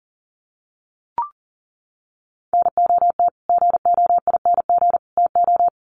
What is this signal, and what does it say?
Morse code, heard as a keyed tone, about 40 wpm, 700 Hz. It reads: NOT GOING TO
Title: Sets of 3 Words 40wpm